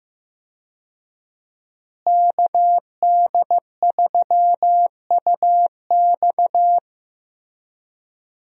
Morse code, heard as a keyed tone, about 15 wpm, 700 Hz. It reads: KD3UX
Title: USA Call Signs - 5 Characters - Repeated 3x - 15wpm